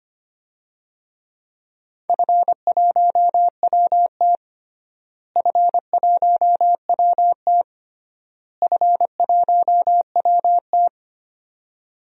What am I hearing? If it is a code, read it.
Morse code, heard as a keyed tone, about 25 wpm, 700 Hz. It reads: F1WT F1WT F1WT
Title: DXCC-100 Call Signs - 4 Characters - Repeated 3x - 25wpm